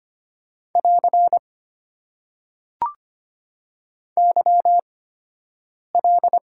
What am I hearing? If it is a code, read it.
Morse code, heard as a keyed tone, about 25 wpm, spaced as for 7 wpm, 690 Hz. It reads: <AR> YL
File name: Top 90 QSO Elements - 8x Character Spacing - 25wpm